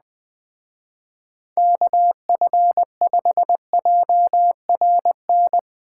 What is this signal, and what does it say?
Morse code, heard as a keyed tone, about 20 wpm, 700 Hz. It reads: KF5JRN
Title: Speed Racing - USA 2x3 Call Signs 17wpm